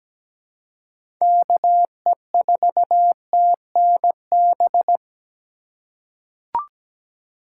Morse code, heard as a keyed tone, about 17 wpm, 700 Hz. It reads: KE4TNB
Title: USA Call Signs - 6 Characters - 2x Character Spacing - 17wpm